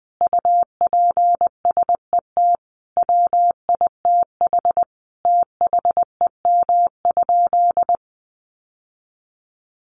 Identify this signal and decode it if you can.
Morse code, heard as a keyed tone, about 20 wpm, 700 Hz. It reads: UPSET WITH THEM?
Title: Sentences from Top 800 Words 20wpm